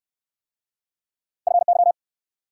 Morse code, heard as a keyed tone, about 62 wpm, 700 Hz. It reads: FY